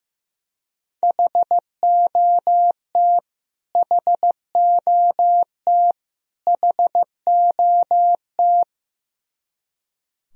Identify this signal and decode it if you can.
Morse code, heard as a keyed tone, about 15 wpm, 700 Hz. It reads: HOT HOT HOT
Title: Top 100 Words Repeated 15wpm